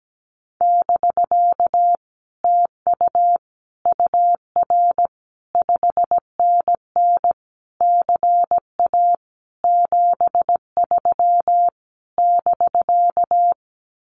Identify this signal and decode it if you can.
Morse code, heard as a keyed tone, about 17 wpm, 700 Hz. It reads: <BK> TU UR 5NN CA 73 <BK>
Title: Common QSO Phrases Encore 17wpm